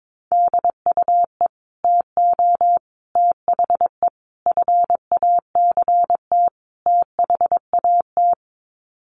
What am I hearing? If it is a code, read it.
Morse code, heard as a keyed tone, about 22 wpm, 700 Hz. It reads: DUE TO THE FACT THAT
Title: Sets of 5 Words Encore 22wpm